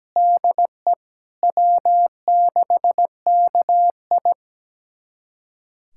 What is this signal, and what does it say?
Morse code, heard as a keyed tone, about 17 wpm, 700 Hz. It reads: DE W6KI
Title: USA Callsign DE Callsign - 4 Characters - 17wpm